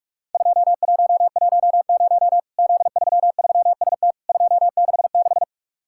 Morse code, heard as a keyed tone, about 48 wpm, 700 Hz. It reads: 2110 Z33ST 266
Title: Worked All Europe DX Contest - QTC Exchange - 45wpm